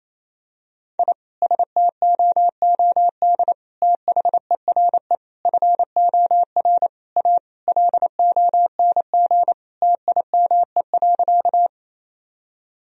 Morse code, heard as a keyed tone, about 27 wpm, 700 Hz. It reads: I STOOD THERE FOR A LONG TIME.
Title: Sentences from Top 1000 Words - Review Part 2 - 28wpm